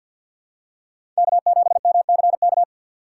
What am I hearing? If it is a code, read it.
Morse code, heard as a keyed tone, about 47 wpm, 700 Hz. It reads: K7MCK
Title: USA Call Signs - 3x Character Spacing - 50wpm